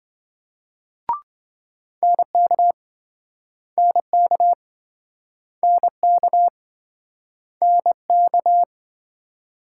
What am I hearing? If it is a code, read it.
Morse code, heard as a keyed tone, about 28 wpm, 700 Hz. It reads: NK NK NK NK